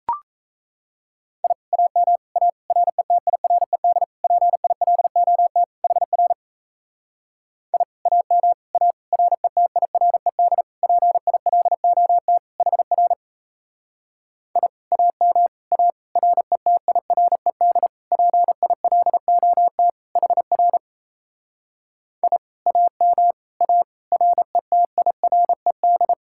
Morse code, heard as a keyed tone, about 38 wpm, 700 Hz. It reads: I AM A RETIRED PILOT HR I AM A RETIRED PILOT HR I AM A RETIRED PILOT HR I AM A RETIRED